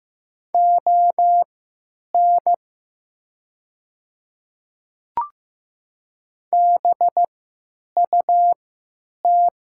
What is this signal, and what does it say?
Morse code, heard as a keyed tone, about 15 wpm, spaced as for 9 wpm, 700 Hz. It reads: ON BUT
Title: 3x Character Spacing - Top 300 Words - 15wpm